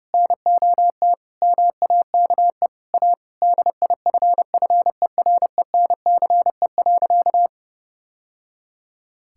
Morse code, heard as a keyed tone, about 30 wpm, 700 Hz. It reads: NOT MAKE A DIFFERENCE.
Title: Sentences from Top 1000 Words - Review Part 5 - 30wpm